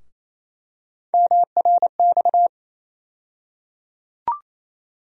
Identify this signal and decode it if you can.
Morse code, heard as a keyed tone, about 28 wpm, 700 Hz. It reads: MRX
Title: USA Call Sign Postfixes - 3 Letters 28wpm